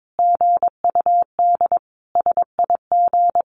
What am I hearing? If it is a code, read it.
Morse code, heard as a keyed tone, about 22 wpm, 700 Hz. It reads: GUD SIG
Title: Common QSO Phrases 22wpm